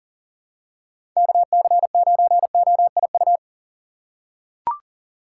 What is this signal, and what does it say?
Morse code, heard as a keyed tone, about 40 wpm, 700 Hz. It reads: KC9OIU
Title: USA Call Signs - 6 Characters - 4x Character Spacing - 40wpm